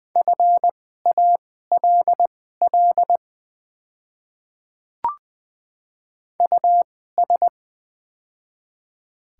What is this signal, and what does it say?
Morse code, heard as a keyed tone, about 20 wpm, spaced as for 14 wpm, 700 Hz. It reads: FALL US
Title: 2x Character Spacing - Top 500 Words - 20wpm